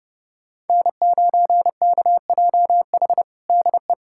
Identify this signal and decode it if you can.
Morse code, heard as a keyed tone, about 30 wpm, 700 Hz. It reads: N9KJH DE
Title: USA Callsign DE Callsign - 5 Characters - 30wpm